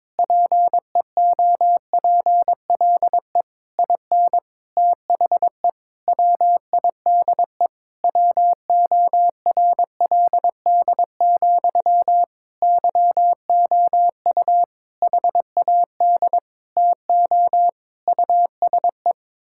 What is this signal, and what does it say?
Morse code, heard as a keyed tone, about 22 wpm, 700 Hz. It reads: PEOPLE IN THE WIDE WORLD, YOU HAD TO USE
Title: Sentences from Top 1000 Words - Review Part 6 - 22wpm